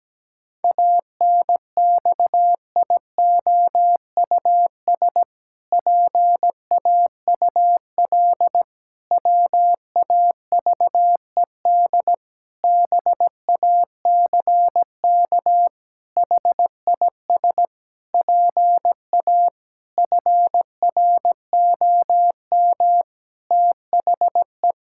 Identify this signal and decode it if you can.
Morse code, heard as a keyed tone, about 17 wpm, 700 Hz. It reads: ANXIOUS PAUL WAVED BACK HIS PA FROM THE
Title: Pangram Challenge v2 - 17wpm